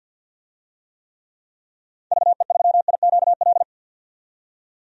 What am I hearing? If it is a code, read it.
Morse code, heard as a keyed tone, about 47 wpm, 700 Hz. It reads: VE3IQL
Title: DXCC-100 Call Signs - 6 Characters - 50wpm